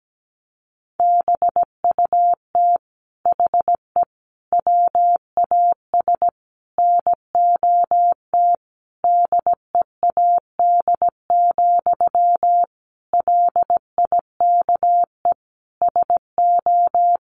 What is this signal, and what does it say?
Morse code, heard as a keyed tone, about 17 wpm, 700 Hz. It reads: BUT HE WAS NOT DEAD, LIKE SO
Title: Sentences from Top 1500 Words 17wpm